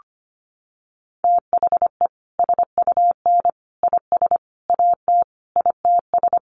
Morse code, heard as a keyed tone, about 25 wpm, 700 Hz. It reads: THE SUN IS AT ITS
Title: Sentences from Top 1300 Words 25wpm